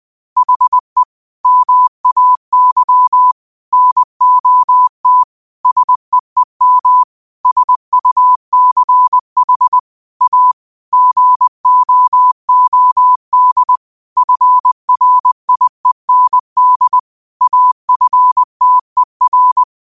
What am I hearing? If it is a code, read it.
Morse code, heard as a keyed tone, about 20 wpm, 1000 Hz. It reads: HE MAY NOT SEEM SUCH A GOOD FRIEND AFTER